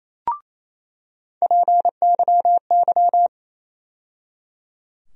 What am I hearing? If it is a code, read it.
Morse code, heard as a keyed tone, about 28 wpm, 700 Hz. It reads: PYY
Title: USA Call Sign Postfixes - 3 Letters 28wpm